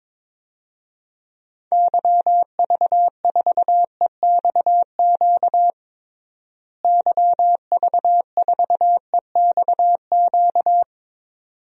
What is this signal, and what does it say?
Morse code, heard as a keyed tone, about 22 wpm, 700 Hz. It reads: YV4EXQ YV4EXQ
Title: DXCC-100 Call Signs - 6 Characters - Repeated 2x - 22wpm